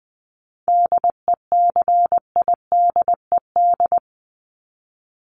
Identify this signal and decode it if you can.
Morse code, heard as a keyed tone, about 20 wpm, 700 Hz. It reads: DECIDED
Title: ED Suffix - Top 100 Words with Base Words 20wpm